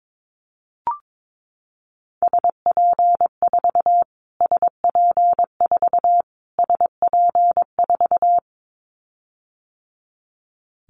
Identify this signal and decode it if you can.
Morse code, heard as a keyed tone, about 22 wpm, 700 Hz. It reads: SP4 SP4 SP4
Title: Callsign Prefixes for Top 25 DXCC - Repeated 3x - 22wpm